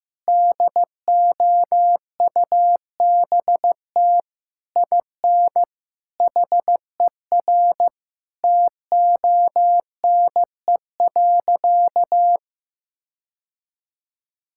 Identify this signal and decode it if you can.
Morse code, heard as a keyed tone, about 15 wpm, 700 Hz. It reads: DOUBT IN HER TONE.
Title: Sentences from Top 1600 Words 15wpm